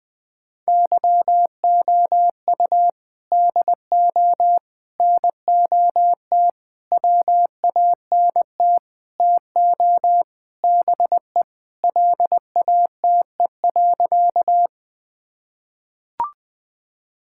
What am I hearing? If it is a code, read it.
Morse code, heard as a keyed tone, about 21 wpm, 700 Hz. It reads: YOU DO NOT WANT TO BE LATE.
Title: Sentences with 5-Word Phrases - Spoken Beginning - 20wpm